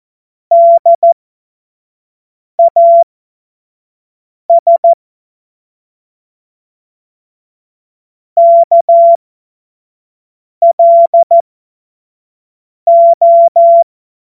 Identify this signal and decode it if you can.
Morse code, heard as a keyed tone, about 14 wpm, spaced as for 5 wpm, 680 Hz. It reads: DAS KLO
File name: Morsekurs / Witze / Witz 16. Dezember 2025 25 F70